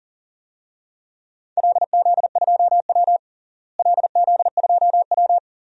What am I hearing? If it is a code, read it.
Morse code, heard as a keyed tone, about 40 wpm, 700 Hz. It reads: LZ2W LZ2W
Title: DXCC-100 Call Signs - 4 Characters - Repeated 2x - 40wpm